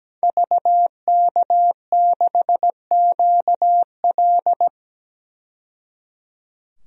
Morse code, heard as a keyed tone, about 17 wpm, 700 Hz. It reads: VK6QL
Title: DXCC-100 Call Signs - 5 Characters - 17wpm